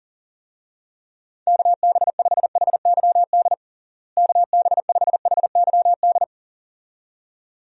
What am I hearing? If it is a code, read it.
Morse code, heard as a keyed tone, about 37 wpm, 700 Hz. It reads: KB5HYD KB5HYD
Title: USA Call Signs - 6 Characters - Repeated 2x - 40wpm